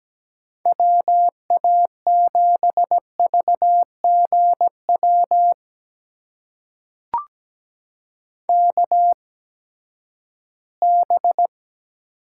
Morse code, heard as a keyed tone, about 17 wpm, spaced as for 12 wpm, 700 Hz. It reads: WA7VGW KB
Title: USA Call Signs - 6 Characters - 8x Character Spacing - 17wpm